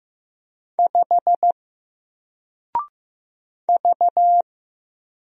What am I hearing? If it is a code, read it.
Morse code, heard as a keyed tone, about 15 wpm, 700 Hz. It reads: 5 V
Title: Single Letter-Number 15wpm